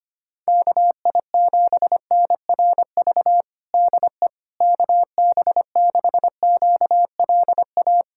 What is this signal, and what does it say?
Morse code, heard as a keyed tone, about 25 wpm, 700 Hz. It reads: KI7NRV DE KB6QLA
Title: USA Callsign DE Callsign 25wpm